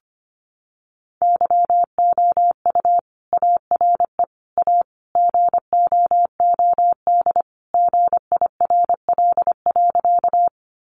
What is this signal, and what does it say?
Morse code, heard as a keyed tone, about 25 wpm, 700 Hz. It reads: YOU ARE A GOOD GIRL.
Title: Sentences from Top 400 Words 25wpm